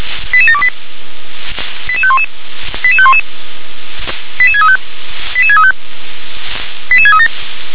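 Five-tone selective call (ZVEI), heard as quick runs of five tones, tones of about 70 ms. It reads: 88728 88511 88511 88635 88635 88637
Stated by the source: ZVEI five-tone selective call received on SDR.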